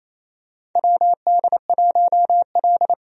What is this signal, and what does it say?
Morse code, heard as a keyed tone, about 28 wpm, 700 Hz. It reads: WD1L
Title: USA Call Signs - 4 Characters - 28wpm